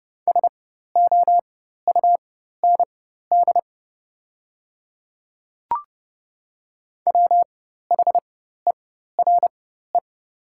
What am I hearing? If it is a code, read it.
Morse code, heard as a keyed tone, about 30 wpm, spaced as for 14 wpm, 700 Hz. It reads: SOUND WHERE
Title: 4x Character Spacing - Top 200 Words - 30wpm